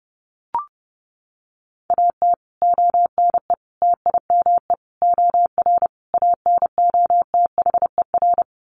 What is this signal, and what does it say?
Morse code, heard as a keyed tone, about 30 wpm, 700 Hz. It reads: AT ONE TIME OR ANOTHER
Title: Sets of 5 Words Encore 30wpm